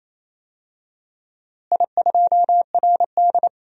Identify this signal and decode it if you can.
Morse code, heard as a keyed tone, about 28 wpm, 700 Hz. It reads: I2RD
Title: DXCC-100 Call Signs - 4 Characters - Repeated 3x - 28wpm